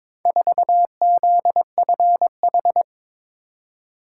Morse code, heard as a keyed tone, about 22 wpm, 700 Hz. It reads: 4ZFH
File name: practice / 22wpm / Four Letter-Number Combinations - 22wpm